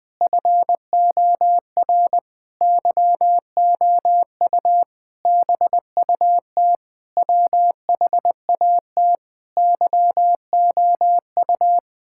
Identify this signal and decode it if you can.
Morse code, heard as a keyed tone, about 20 wpm, 700 Hz. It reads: FOR YOU BUT WHAT YOU
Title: Sentences with 9-Word Phrases - 20wpm